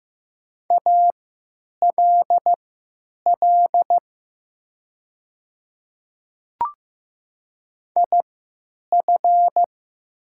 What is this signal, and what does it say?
Morse code, heard as a keyed tone, about 15 wpm, spaced as for 9 wpm, 700 Hz. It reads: ALL IF
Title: 3x Character Spacing - Top 200 Words - 15wpm